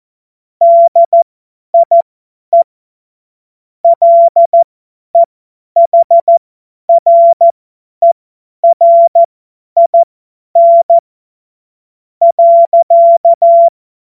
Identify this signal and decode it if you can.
Morse code, heard as a keyed tone, about 14 wpm, spaced as for 9 wpm, 680 Hz. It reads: DIE LEHRERIN.
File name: Morsekurs / Witze / Witz 13. September 2022 50 F70